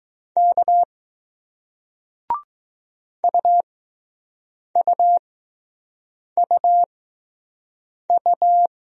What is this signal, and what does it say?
Morse code, heard as a keyed tone, about 20 wpm, spaced as for 6 wpm, 700 Hz. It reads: K UUUU